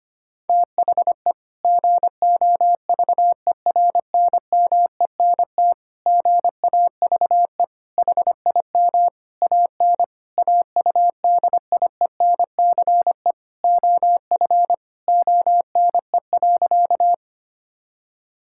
Morse code, heard as a keyed tone, about 25 wpm, 700 Hz. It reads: THE GOVERNMENT GAVE HIM AN AUDIENCE OF ONE.
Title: Sentences from Top 1000 Words - Review Part 5 - 25wpm